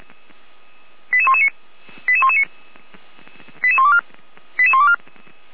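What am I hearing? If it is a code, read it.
Five-tone selective call (ZVEI), heard as quick runs of five tones, tones of about 80 ms. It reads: 88119 88119 88124 88124
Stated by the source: ZVEI five-tone selective call 88093 received with an SDR, Feuerwehr Orlamünde.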